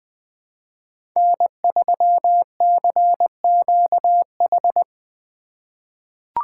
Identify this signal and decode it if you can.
Morse code, heard as a keyed tone, about 21 wpm, 700 Hz. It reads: N3CQH
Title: USA Call Signs - 3x Character Spacing - 20wpm